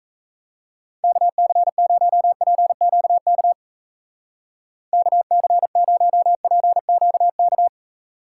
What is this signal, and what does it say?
Morse code, heard as a keyed tone, about 34 wpm, 700 Hz. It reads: KC0PQK KC0PQK
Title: Speed Racing - USA Call Signs 28wpm